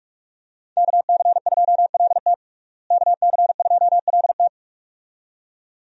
Morse code, heard as a keyed tone, about 46 wpm, 700 Hz. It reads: KC2LT KC2LT
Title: USA Call Signs - 5 Characters - Repeated 2x - 45wpm